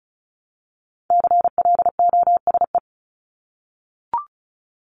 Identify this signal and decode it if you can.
Morse code, heard as a keyed tone, about 34 wpm, 700 Hz. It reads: CLOSE
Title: LY Suffix - Top 100 Words with Base Words 35wpm